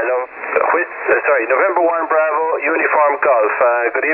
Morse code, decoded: none